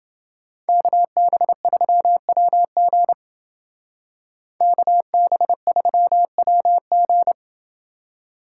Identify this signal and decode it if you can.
Morse code, heard as a keyed tone, about 28 wpm, 700 Hz. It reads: KB3WG KB3WG